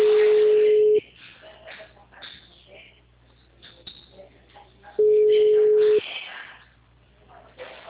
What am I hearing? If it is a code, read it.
Ringback tone (Europe, 425 Hz), with speech.
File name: out- -2050- - - .12303